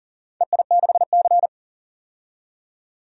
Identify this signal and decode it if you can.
Morse code, heard as a keyed tone, about 40 wpm, 700 Hz. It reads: EI6C
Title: DXCC-100 Call Signs - 4 Characters - 40wpm